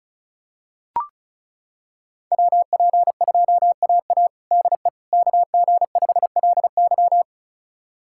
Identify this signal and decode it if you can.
Morse code, heard as a keyed tone, about 34 wpm, 700 Hz. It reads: WP2AA DE KG5LY
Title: USA Callsign DE Callsign - 5 Characters - 35wpm